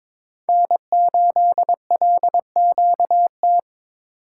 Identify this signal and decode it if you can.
Morse code, heard as a keyed tone, about 22 wpm, 700 Hz. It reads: N8LQT
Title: USA Call Signs - 5 Characters - 8x Character Spacing - 22wpm